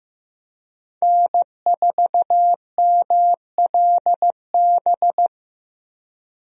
Morse code, heard as a keyed tone, about 16 wpm, 700 Hz. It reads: N4MLB
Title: USA Call Signs - 5 Characters - Repeated 3x - 15wpm